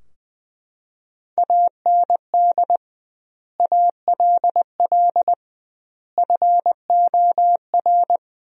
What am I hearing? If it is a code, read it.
Morse code, heard as a keyed tone, about 21 wpm, 700 Hz. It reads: AND ALL FOR
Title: 2x Word-Spacing - Sentences from Top 100 Words - 20wpm